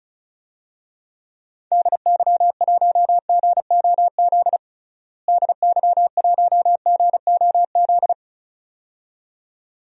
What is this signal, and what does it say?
Morse code, heard as a keyed tone, about 36 wpm, 700 Hz. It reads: DY1GOZ DY1GOZ
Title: DXCC-100 Call Signs - 6 Characters - Repeated 2x - 35wpm